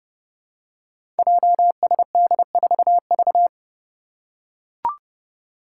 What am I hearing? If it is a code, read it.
Morse code, heard as a keyed tone, about 30 wpm, 700 Hz. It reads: JSD4V
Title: Five Challenging Letter-Number Combinations 30wpm